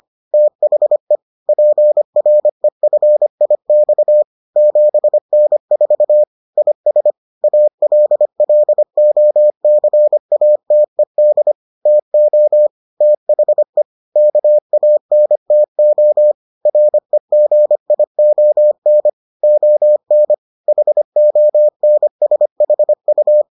Morse code, heard as a keyed tone, about 25 wpm, 600 Hz. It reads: THE PREFIX 7N4 IS ALLOCATED TO THE KANTO REGION ON HONSHU